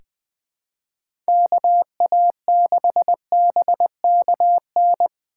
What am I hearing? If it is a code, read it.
Morse code, heard as a keyed tone, about 20 wpm, 700 Hz. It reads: KA6BKN
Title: USA Call Signs - 6 Characters - 3x Character Spacing - 20wpm